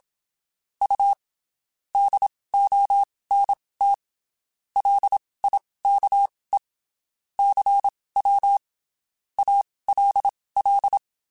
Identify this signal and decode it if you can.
Morse code, heard as a keyed tone, about 27 wpm, spaced as for 19 wpm, 800 Hz. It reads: U DONT LIKE CW ALL
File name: (spaced) / QSOs Spaced-mp3s / WD0FBT @25 sp